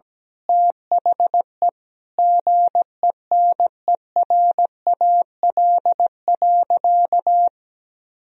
Morse code, heard as a keyed tone, about 17 wpm, 700 Hz. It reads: THE GENERAL.
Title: Sentences from Top 1000 Words - Review Part 3 - 17wpm